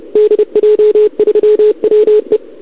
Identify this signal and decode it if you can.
Morse code, heard as a keyed tone, about 30 wpm, 440 Hz. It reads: DJ3WE
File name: CQWWCW2021 / Doderna OL3Z